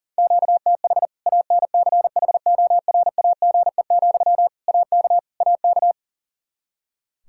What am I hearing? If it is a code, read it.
Morse code, heard as a keyed tone, about 40 wpm, 700 Hz. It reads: QTH ANCHORAGE, AK AK
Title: Practice QSO - Exchange 1 - Beginning - 40wpm